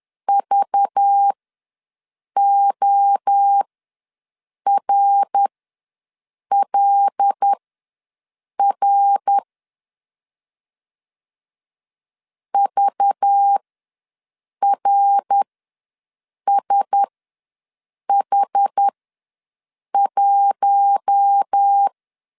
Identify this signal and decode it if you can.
Morse code, heard as a keyed tone, about 10 wpm, spaced as for 6 wpm, 800 Hz. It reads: VORLR VRSH1
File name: LEZIONI / 5 LBVC / 5 LBVC ws 10wpm 3733